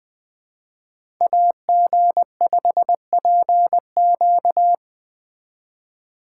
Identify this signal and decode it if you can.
Morse code, heard as a keyed tone, about 20 wpm, 700 Hz. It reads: AG5PQ
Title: Speed Racing - USA 2x2 Call Signs 17wpm